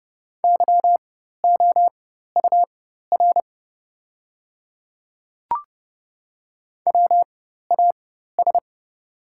Morse code, heard as a keyed tone, about 30 wpm, spaced as for 14 wpm, 690 Hz. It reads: YOUR WAS
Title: 4x Character Spacing - Top 100 Words - 30wpm